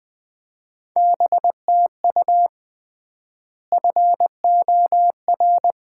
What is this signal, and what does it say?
Morse code, heard as a keyed tone, about 20 wpm, 700 Hz. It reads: BTU FOR
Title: Common QSO Phrases - 3x Word-Spacing - 20wpm